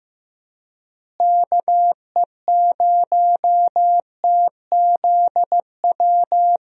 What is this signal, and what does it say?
Morse code, heard as a keyed tone, about 15 wpm, 700 Hz. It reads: KE0TZW
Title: USA Call Signs - 6 Characters - Repeated 3x - 15wpm